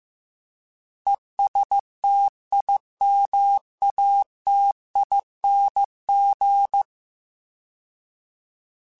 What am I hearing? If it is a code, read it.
Morse code, heard as a keyed tone, about 15 wpm, 800 Hz. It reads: ESTIMATING
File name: Alphabetical / Alphabetical mp3s / Es @13